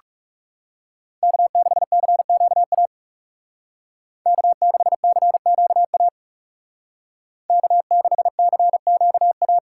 Morse code, heard as a keyed tone, about 36 wpm, 700 Hz. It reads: K6CQA K6CQA K6CQA
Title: Speed Racing - USA 1x3 Call Signs 30wpm